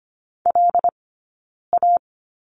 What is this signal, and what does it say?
Morse code, heard as a keyed tone, about 25 wpm, spaced as for 9 wpm, 690 Hz. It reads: LA
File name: Lesson 15d - Number 5 - Two Characters 25wpm